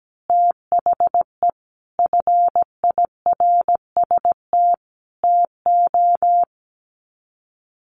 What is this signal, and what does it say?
Morse code, heard as a keyed tone, about 17 wpm, 700 Hz. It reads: THE FIRST TO
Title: Sets of 5 Words Encore 17wpm